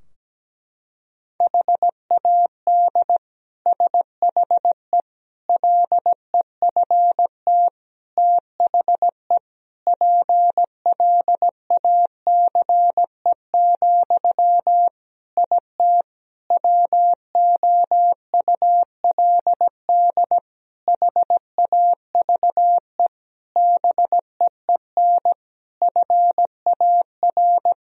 Morse code, heard as a keyed tone, about 17 wpm, 700 Hz. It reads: HAD SHE LEFT THE PLACE, IT WOULD HAVE BEEN FAR
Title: Sentences from Top 1000 Words - Review Part 3 - 17wpm